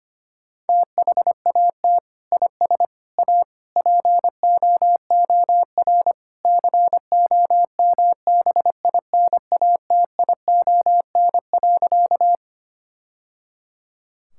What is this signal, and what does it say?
Morse code, heard as a keyed tone, about 25 wpm, 700 Hz. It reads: THAT IS A POOR COMBINATION.
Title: Sentences from Top 1000 Words - Review Part 6 - 25wpm